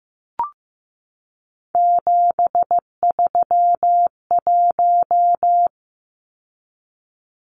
Morse code, heard as a keyed tone, about 15 wpm, 700 Hz. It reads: 731
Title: Numbers - 3-Digit 15wpm